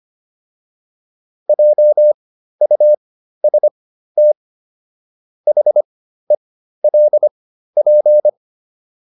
Morse code, heard as a keyed tone, about 25 wpm, spaced as for 13 wpm, 600 Hz. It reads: JUST HELP